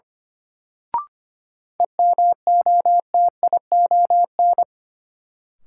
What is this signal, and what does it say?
Morse code, heard as a keyed tone, about 25 wpm, 700 Hz. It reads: EMOTION
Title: Sentences from Top 1600 Words - New Word Drill - 25wpm